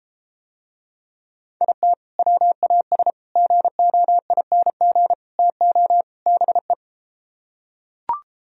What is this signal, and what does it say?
Morse code, heard as a keyed tone, about 33 wpm, 700 Hz. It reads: IT WAS GOING TO BE